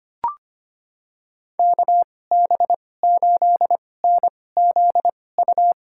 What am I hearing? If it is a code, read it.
Morse code, heard as a keyed tone, about 25 wpm, spaced as for 18 wpm, 700 Hz. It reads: KB8NZU
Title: USA Call Signs - 2x Character Spacing - 25wpm